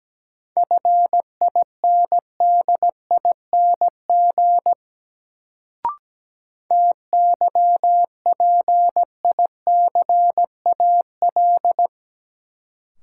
Morse code, heard as a keyed tone, about 17 wpm, 700 Hz. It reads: FINDING TYPICAL
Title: Sentences from Top 1100 Words - New Word Drill Alt - 17wpm